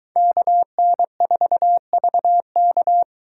Morse code, heard as a keyed tone, about 22 wpm, 700 Hz. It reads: KN4VK